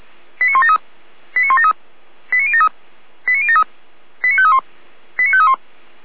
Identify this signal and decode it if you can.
Five-tone selective call (ZVEI), heard as quick runs of five tones, tones of about 70 ms. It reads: 89173 89173 89073 89073 89531 89531